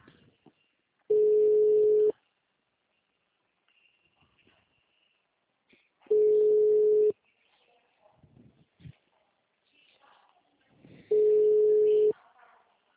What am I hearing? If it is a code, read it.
Ringback tone (Europe, 425 Hz).
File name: out- -2048- - - .39